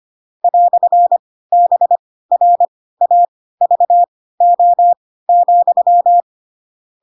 Morse code, heard as a keyed tone, about 25 wpm, spaced as for 16 wpm, 700 Hz. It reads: "BRAVO,
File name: Pinocchio / Pinocchio ch2 10 25 ik1hge